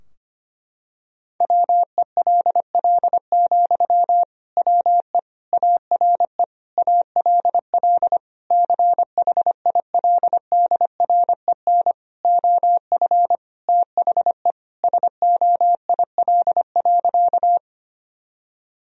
Morse code, heard as a keyed tone, about 25 wpm, 700 Hz. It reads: WELL, WE ARE ALL CHILDREN OF THE SOIL.
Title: Sentences from Top 1000 Words - Review Part 4 - 25wpm